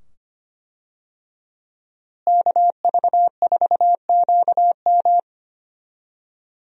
Morse code, heard as a keyed tone, about 25 wpm, 700 Hz. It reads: KV4QM
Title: USA Call Signs - 5 Characters - Repeated 2x - 25wpm